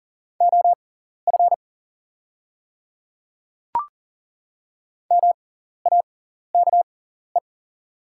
Morse code, heard as a keyed tone, about 40 wpm, spaced as for 14 wpm, 690 Hz. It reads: OF MAKE
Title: 6x Character Spacing - Top 100 Words - 40wpm